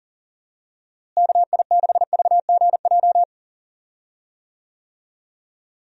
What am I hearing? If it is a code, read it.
Morse code, heard as a keyed tone, about 37 wpm, 700 Hz. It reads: KI6VGJ
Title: USA Call Signs - 6 Characters - Repeated 2x - 40wpm